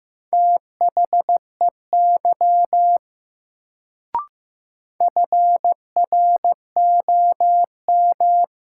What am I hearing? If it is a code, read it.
Morse code, heard as a keyed tone, about 15 wpm, 700 Hz. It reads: THEY FROM
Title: Sets of 2 Words - New Word Drill - 15wpm